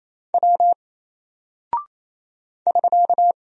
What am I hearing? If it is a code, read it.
Morse code, heard as a keyed tone, about 27 wpm, 700 Hz. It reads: W <SK>
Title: Single Character (Letter:Number:Punctuation:Prosign) 28wpm